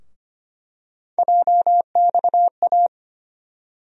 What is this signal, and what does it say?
Morse code, heard as a keyed tone, about 25 wpm, 700 Hz. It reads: JXA
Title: USA Call Sign Postfixes - 3 Letters 25wpm